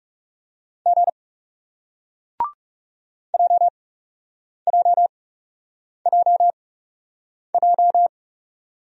Morse code, heard as a keyed tone, about 40 wpm, spaced as for 8 wpm, 700 Hz. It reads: G JJJJ